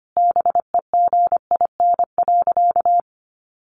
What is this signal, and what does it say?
Morse code, heard as a keyed tone, about 25 wpm, 700 Hz. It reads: BEGIN.
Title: Sentences from Top 300 Words 25wpm